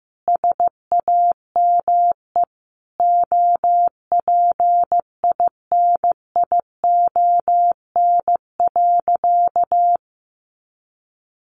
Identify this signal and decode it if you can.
Morse code, heard as a keyed tone, about 15 wpm, 700 Hz. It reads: SAME OPINION.
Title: Sentences from Top 1200 Words 15wpm